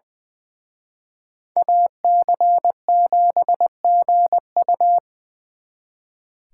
Morse code, heard as a keyed tone, about 25 wpm, 700 Hz. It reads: AC7GU
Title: Speed Racing - USA Call Signs 20wpm